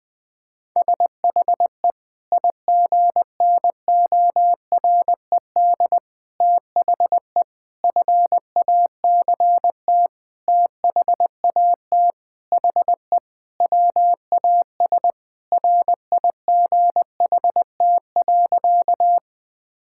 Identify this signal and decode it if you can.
Morse code, heard as a keyed tone, about 20 wpm, 700 Hz. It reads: SHE IGNORED THE FACT THAT HE WAS RIGHT.
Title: Sentences with 5-Word Phrases - 20wpm